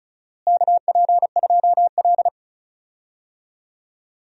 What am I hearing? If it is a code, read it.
Morse code, heard as a keyed tone, about 36 wpm, 700 Hz. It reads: KP2L
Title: DXCC-100 Call Signs - 4 Characters - Repeated 2x - 35wpm